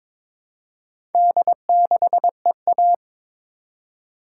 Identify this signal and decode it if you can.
Morse code, heard as a keyed tone, about 22 wpm, 700 Hz. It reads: D6EA
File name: practice / 22wpm / Four Letter-Number Combinations - 22wpm